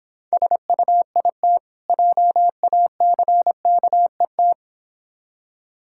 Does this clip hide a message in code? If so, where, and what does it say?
Yes: Morse code from 0.3 to 4.5 s, heard as a keyed tone, about 26 wpm, 700 Hz. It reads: SUIT JACKET